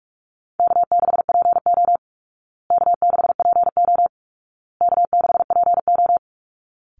Morse code, heard as a keyed tone, about 44 wpm, 700 Hz. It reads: K6PO K6PO K6PO
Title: Call Signs Repeated 2x - 45wpm